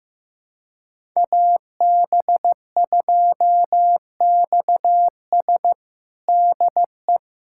Morse code, heard as a keyed tone, about 16 wpm, 700 Hz. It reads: AB2XS DE
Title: USA Callsign DE Callsign - 5 Characters - 15wpm